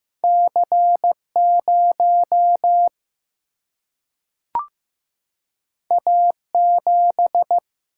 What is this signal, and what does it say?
Morse code, heard as a keyed tone, about 15 wpm, 700 Hz. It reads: C0 A7
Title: Single-Letter Single-Number 15wpm